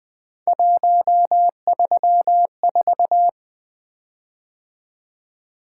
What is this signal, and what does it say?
Morse code, heard as a keyed tone, about 20 wpm, 700 Hz. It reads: 134
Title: Three Letter-Number Combinations - 20wpm